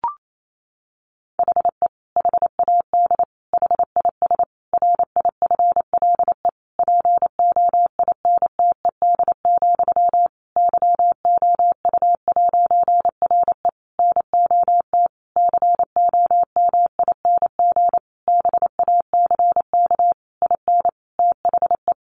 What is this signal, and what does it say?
Morse code, heard as a keyed tone, about 28 wpm, 700 Hz. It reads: HE HAD HIS RIFLE POINTED, YOU'RE NOT COMING BACK IN THE
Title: A Sound Of Thunder 28wpm